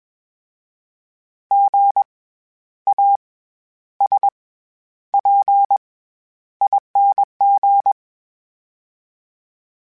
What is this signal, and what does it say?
Morse code, heard as a keyed tone, about 21 wpm, spaced as for 11 wpm, 800 Hz. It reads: GASPING
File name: Special INGs-mp3s / Special INGs-mp3s / SlowToFaster ing @20